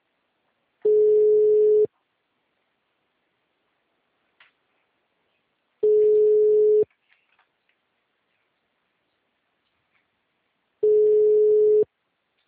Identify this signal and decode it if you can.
Ringback tone (Europe, 425 Hz).
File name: out- -2008- - - .213